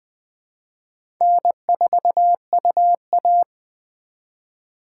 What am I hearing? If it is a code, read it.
Morse code, heard as a keyed tone, about 20 wpm, 700 Hz. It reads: N4UA
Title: USA Call Signs - 2x Character Spacing - 20wpm